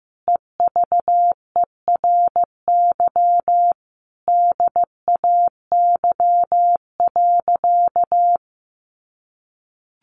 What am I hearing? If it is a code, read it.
Morse code, heard as a keyed tone, about 15 wpm, 700 Hz. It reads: EVERY DAY.
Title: Sentences from Top 1100 Words 15wpm